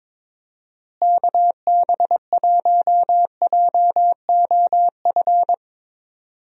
Morse code, heard as a keyed tone, about 22 wpm, 700 Hz. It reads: KB1JOF
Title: USA Call Signs - 6 Characters - Repeated 2x - 22wpm